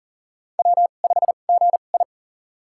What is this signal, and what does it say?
Morse code, heard as a keyed tone, about 40 wpm, spaced as for 29 wpm, 700 Hz. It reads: W5GI
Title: USA Call Signs - 4 Characters - 2x Character Spacing - 40wpm